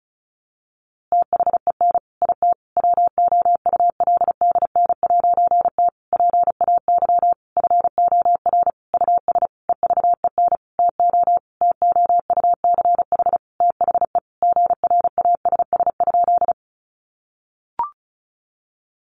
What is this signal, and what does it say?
Morse code, heard as a keyed tone, about 34 wpm, 700 Hz. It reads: THEN IT WOULDN'T PAY FOR US EVEN TO TOUCH THE GRASS?
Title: A Sound Of Thunder 35wpm